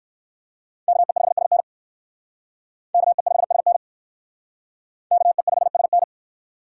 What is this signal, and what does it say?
Morse code, heard as a keyed tone, about 58 wpm, 700 Hz. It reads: KE5SN KE5SN KE5SN